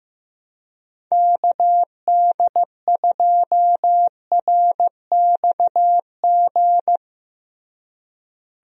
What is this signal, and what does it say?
Morse code, heard as a keyed tone, about 15 wpm, 700 Hz. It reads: KD2RXG
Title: USA Call Signs - 6 Characters - Repeated 2x - 15wpm